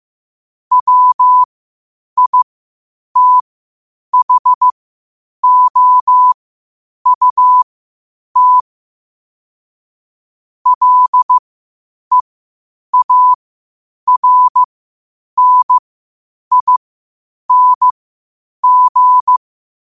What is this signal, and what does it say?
Morse code, heard as a keyed tone, about 15 wpm, spaced as for 8 wpm, 1000 Hz. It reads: WITHOUT LEARNING
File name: qotdfiles / 8wpm